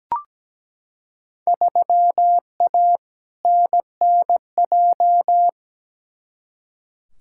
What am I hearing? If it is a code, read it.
Morse code, heard as a keyed tone, about 17 wpm, 700 Hz. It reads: 3A NNJ
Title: Field Day Exchanges - 17wpm